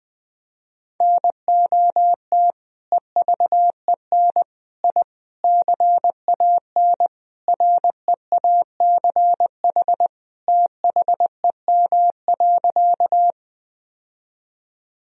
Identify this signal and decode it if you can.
Morse code, heard as a keyed tone, about 21 wpm, 700 Hz. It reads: NOT EVEN I CAN REACH THEM.
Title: Sentences from Top 1000 Words - Review Part 2 - 20wpm